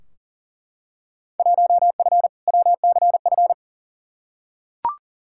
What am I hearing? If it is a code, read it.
Morse code, heard as a keyed tone, about 40 wpm, 700 Hz. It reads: 1F WCF
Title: Field Day Exchanges - 40wpm